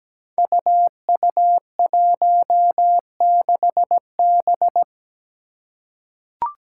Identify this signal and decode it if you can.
Morse code, heard as a keyed tone, about 17 wpm, 700 Hz. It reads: UU16B
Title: Five Challenging Letter-Number Combinations 17wpm